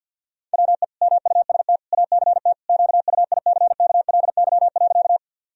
Morse code, heard as a keyed tone, about 46 wpm, 700 Hz. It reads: WE MUST ACT QUICKLY.
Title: Sentences from Top 1000 Words - Review Part 1 - 50wpm